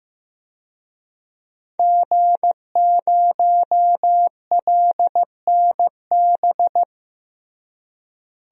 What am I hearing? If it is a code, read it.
Morse code, heard as a keyed tone, about 16 wpm, 700 Hz. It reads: G0LNB
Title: DXCC-100 Call Signs - 5 Characters - Repeated 3x - 15wpm